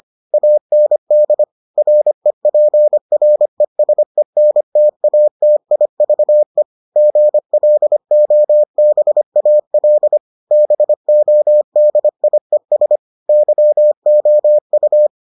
Morse code, heard as a keyed tone, about 25 wpm, 600 Hz. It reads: AND REPRESENTATIVE GLOBAL BODIES YOU